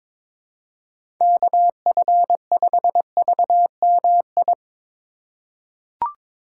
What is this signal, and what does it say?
Morse code, heard as a keyed tone, about 22 wpm, 700 Hz. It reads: KF5VMI
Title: USA Call Signs - 6 Characters - 3x Character Spacing - 22wpm